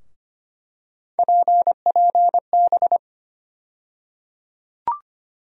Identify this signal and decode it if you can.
Morse code, heard as a keyed tone, about 25 wpm, 700 Hz. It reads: PPB
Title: USA Call Sign Postfixes - 3 Letters 25wpm